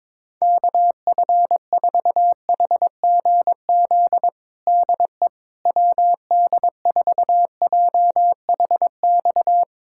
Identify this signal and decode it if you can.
Morse code, heard as a keyed tone, about 22 wpm, 700 Hz. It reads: KF4HGZ DE WD4JHX
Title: USA Callsign DE Callsign - 6 Characters - 22wpm